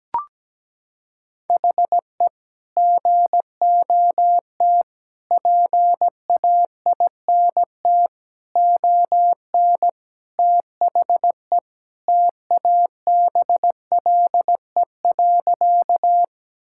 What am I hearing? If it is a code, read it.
Morse code, heard as a keyed tone, about 17 wpm, 700 Hz. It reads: HE GOT PAINT ON THE TABLE.
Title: Sentences from Top 1000 Words - Review Part 6 - 17wpm